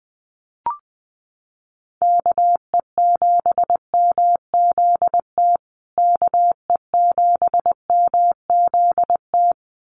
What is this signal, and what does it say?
Morse code, heard as a keyed tone, about 19 wpm, 700 Hz. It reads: KE7MZT KE7MZT
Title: USA Call Signs Repeated 2x 20wpm